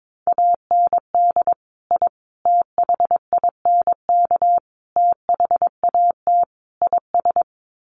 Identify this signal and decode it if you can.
Morse code, heard as a keyed tone, about 22 wpm, 700 Hz. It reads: AND I THINK THAT IS
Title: Sets of 5 Words Encore2 22wpm